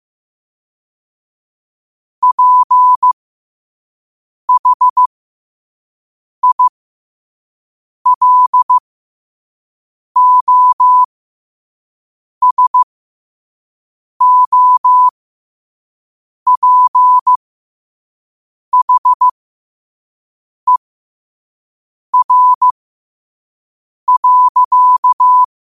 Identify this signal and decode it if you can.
Morse code, heard as a keyed tone, about 15 wpm, spaced as for 5 wpm, 1000 Hz. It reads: PHILOSOPHER.
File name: qotdfiles / 5wpm